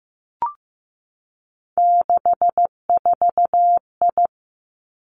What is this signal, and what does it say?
Morse code, heard as a keyed tone, about 15 wpm, 700 Hz. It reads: 64I
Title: Three Challenging Letter-Number Combinations 15wpm